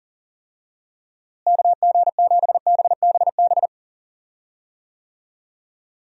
Morse code, heard as a keyed tone, about 40 wpm, 700 Hz. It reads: KG7BBB
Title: USA Call Signs - 6 Characters - Repeated 2x - 40wpm